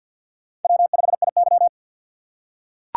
Morse code, heard as a keyed tone, about 45 wpm, 700 Hz. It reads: W5IY
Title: USA Call Signs - 4x Character Spacing - 50wpm